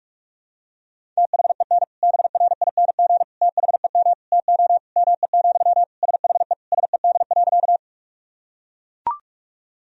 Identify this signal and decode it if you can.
Morse code, heard as a keyed tone, about 42 wpm, 700 Hz. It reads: THEN BRING THEM TO ME, SHE SED.
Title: Sentences from Top 500 Words and Abbreviations 45wpm